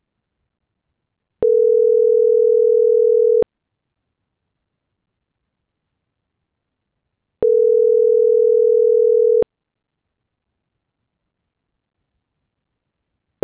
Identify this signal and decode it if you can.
Ringback tone (North America), 440 + 480 Hz.